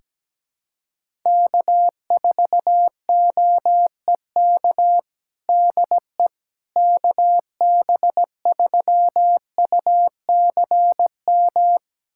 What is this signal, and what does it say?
Morse code, heard as a keyed tone, about 17 wpm, 700 Hz. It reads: K4OEK DE KB3UCM
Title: USA Callsign DE Callsign 17wpm